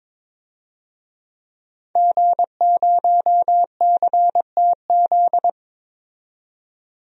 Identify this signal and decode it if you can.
Morse code, heard as a keyed tone, about 22 wpm, 700 Hz. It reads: G0CTZ
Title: DXCC-100 Call Signs - 5 Characters - 22wpm